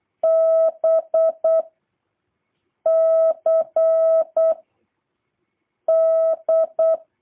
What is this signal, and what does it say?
Morse code, heard as a keyed tone, about 8 wpm, 650 Hz. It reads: B C D